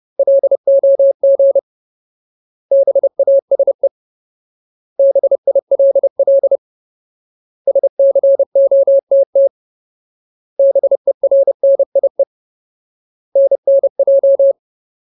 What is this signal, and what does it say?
Morse code, heard as a keyed tone, about 30 wpm, 550 Hz. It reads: LOG BASE BILL SCOTT BERNIE NNJ
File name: Ultimate Word List / 30WPM Ultimate